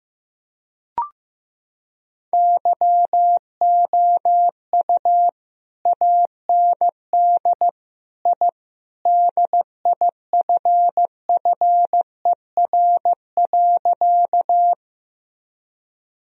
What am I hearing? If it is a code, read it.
Morse code, heard as a keyed tone, about 15 wpm, 700 Hz. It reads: YOU AND I DIFFER.
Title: Sentences from Top 1000 Words - Review Part 1 - 15wpm